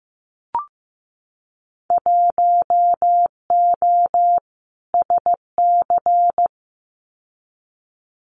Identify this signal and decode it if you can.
Morse code, heard as a keyed tone, about 15 wpm, 700 Hz. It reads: 1O SC
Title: Winter Field Day Exchanges 15wpm